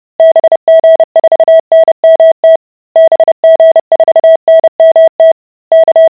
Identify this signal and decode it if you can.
Morse code, heard as a keyed tone, about 30 wpm, 650 Hz. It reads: BG4NMT BG4NMT K